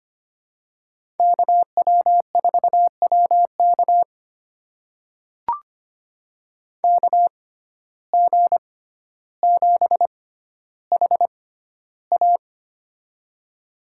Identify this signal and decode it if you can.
Morse code, heard as a keyed tone, about 25 wpm, spaced as for 13 wpm, 700 Hz. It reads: KW4WK KG7HA
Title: USA Call Signs - 5 Characters - 6x Character Spacing - 25wpm